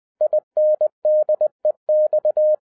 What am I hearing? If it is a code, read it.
Morse code, heard as a keyed tone, about 20 wpm, 610 Hz. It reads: INDEX
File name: 2011-07-mp3 / n23 QRP OTAKU 20wpm